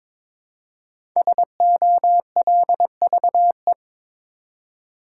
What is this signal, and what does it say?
Morse code, heard as a keyed tone, about 23 wpm, 700 Hz. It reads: SOLVE
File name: 5 Letter Words - First 3 Letters Given - 22wpm